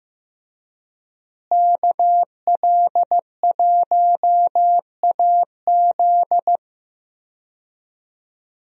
Morse code, heard as a keyed tone, about 15 wpm, 700 Hz. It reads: KL1AZ
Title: USA Call Signs - 5 Characters - Repeated 3x - 15wpm